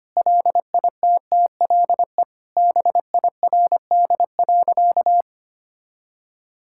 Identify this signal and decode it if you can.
Morse code, heard as a keyed tone, about 25 wpm, 700 Hz. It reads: LITTLE BIRD.
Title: Sentences from Top 1000 Words - Review Part 6 - 25wpm